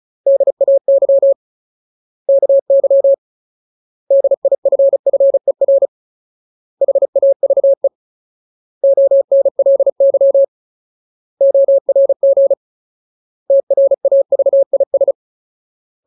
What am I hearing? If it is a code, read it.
Morse code, heard as a keyed tone, about 34 wpm, 550 Hz. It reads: DAY KY DIFFER HAVE ONLY ORG TRAVIS
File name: Ultimate Word List / 35WPM Ultimate